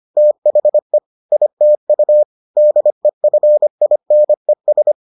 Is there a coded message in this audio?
Yes: Morse code from 0.2 to 4.9 s, heard as a keyed tone, about 25 wpm, 600 Hz. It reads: THE ITU DEFINES